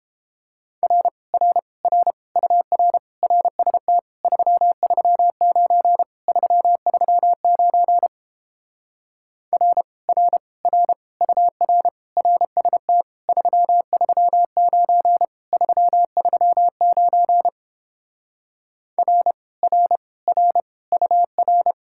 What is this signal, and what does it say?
Morse code, heard as a keyed tone, about 29 wpm, 700 Hz. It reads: R R R UR RST 339 339 R R R UR RST 339 339 R R R UR